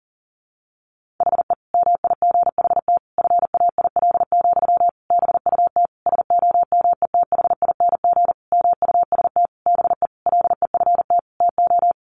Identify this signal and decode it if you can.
Morse code, heard as a keyed tone, about 40 wpm, 700 Hz. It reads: HE MIGHT FAIL, BUT SOMETHING MUST BE LEFT TO
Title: Sentences from Top 1100 Words 40wpm